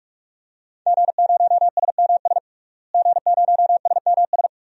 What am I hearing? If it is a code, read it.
Morse code, heard as a keyed tone, about 46 wpm, 700 Hz. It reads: G0SMS G0SMS
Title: DXCC-100 Call Signs - 5 Characters - Repeated 2x - 45wpm